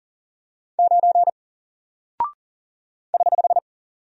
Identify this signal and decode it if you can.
Morse code, heard as a keyed tone, about 39 wpm, 700 Hz. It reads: 9 <HH>
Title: Single Character (Letter:Number:Punctuation:Prosign) 40wpm